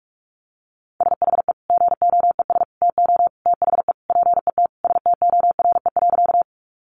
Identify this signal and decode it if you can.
Morse code, heard as a keyed tone, about 44 wpm, 700 Hz. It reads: SHE GOES TO THE PET STORE.
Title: Easy Phrases 45wpm